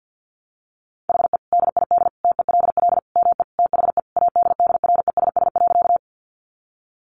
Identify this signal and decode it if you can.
Morse code, heard as a keyed tone, about 48 wpm, 700 Hz. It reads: HE DID TELL ME THE ADDRESS.
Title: Sentences from Top 1000 Words 50wpm